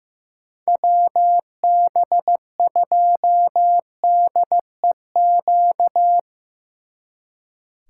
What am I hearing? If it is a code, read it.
Morse code, heard as a keyed tone, about 17 wpm, 700 Hz. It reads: WB2DEQ
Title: Speed Racing - USA 2x3 Call Signs 15wpm